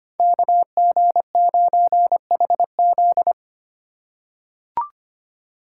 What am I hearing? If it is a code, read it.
Morse code, heard as a keyed tone, about 25 wpm, 700 Hz. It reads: KG9HZ
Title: USA Call Signs - 5 Characters - 3x Character Spacing - 25wpm